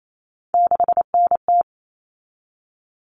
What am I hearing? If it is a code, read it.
Morse code, heard as a keyed tone, about 28 wpm, 700 Hz. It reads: 6NT
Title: All Cut Numbers - 3-Digit 28wpm